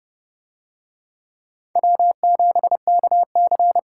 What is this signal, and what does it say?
Morse code, heard as a keyed tone, about 28 wpm, 700 Hz. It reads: W7KC